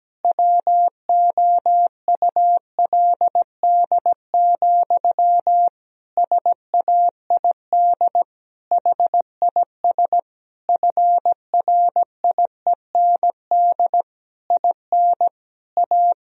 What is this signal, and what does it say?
Morse code, heard as a keyed tone, about 17 wpm, 700 Hz. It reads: WOULD, SAID HIS FRIEND IN A
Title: Sentences from Top 1000 Words - Review Part 1 - 17wpm